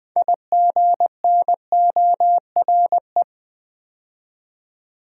Morse code, heard as a keyed tone, about 20 wpm, 700 Hz. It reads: IGNORE
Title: 6 Letter Words - First 3 Letters Given - 20wpm